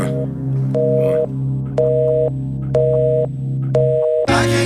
Busy tone (North America), 480 + 620 Hz, with noise.